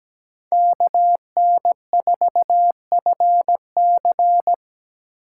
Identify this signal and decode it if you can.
Morse code, heard as a keyed tone, about 17 wpm, 700 Hz. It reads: KN4FC
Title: USA Call Signs - 5 Characters - 3x Character Spacing - 17wpm